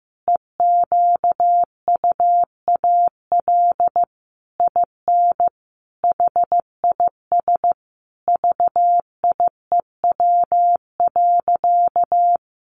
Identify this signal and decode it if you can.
Morse code, heard as a keyed tone, about 14 wpm, 700 Hz. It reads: EQUAL IN HIS VIEW.
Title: Sentences from Top 1400 Words 15wpm